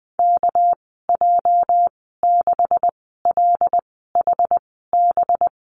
Morse code, heard as a keyed tone, about 20 wpm, spaced as for 14 wpm, 700 Hz. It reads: KJ6LHB
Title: 219 Lesson 35g - Number 0 - Call Signs 20wpm